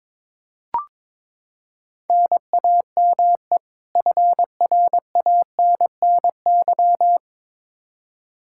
Morse code, heard as a keyed tone, about 22 wpm, 700 Hz. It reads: NAME FRANNY
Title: Practice QSO - Exchange 1 - Beginning - 22wpm